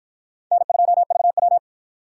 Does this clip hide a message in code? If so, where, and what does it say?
Yes: Morse code from 0.5 to 1.6 s, heard as a keyed tone, about 53 wpm, 700 Hz. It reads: N2VW